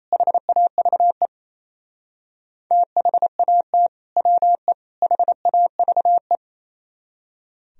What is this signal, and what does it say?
Morse code, heard as a keyed tone, about 35 wpm, 700 Hz. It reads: HAVE THAT WE HAVE
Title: Speed-Racing - Sets of 3 Words 28wpm